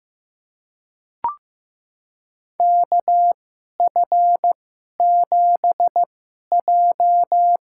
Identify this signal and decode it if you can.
Morse code, heard as a keyed tone, about 15 wpm, spaced as for 11 wpm, 700 Hz. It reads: KF7J
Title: USA Call Signs - 6 Characters - 2x Character Spacing - 15wpm